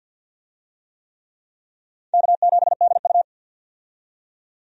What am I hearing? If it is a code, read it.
Morse code, heard as a keyed tone, about 51 wpm, 700 Hz. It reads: K7DU
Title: USA Call Signs - 4 Characters - 50wpm